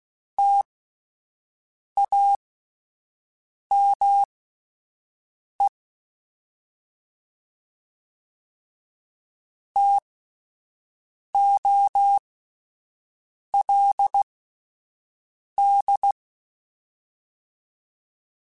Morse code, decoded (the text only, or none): TAME TOLD